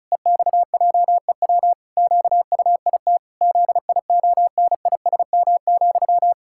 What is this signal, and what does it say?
Morse code, heard as a keyed tone, about 36 wpm, 700 Hz. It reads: EXJEW QUIT ZIONISM,
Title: Pangram Challenge v2 - 35wpm